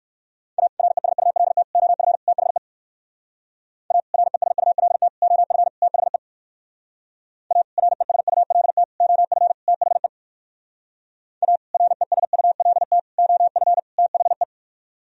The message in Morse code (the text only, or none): A RESULT OF THE A RESULT OF THE A RESULT OF THE A RESULT OF THE